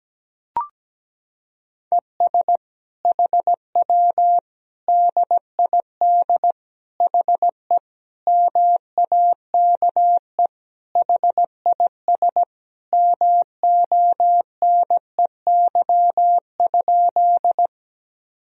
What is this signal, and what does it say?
Morse code, heard as a keyed tone, about 17 wpm, 700 Hz. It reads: ES HW DID HE MAKE HIS MONEY?
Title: Sentences from Top 400 Words and Abbreviations 17wpm